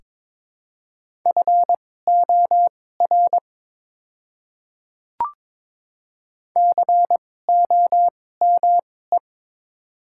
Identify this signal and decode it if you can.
Morse code, heard as a keyed tone, about 22 wpm, spaced as for 16 wpm, 700 Hz. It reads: FOR COME
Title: 2x Character Spacing - Top 100 Words - 22wpm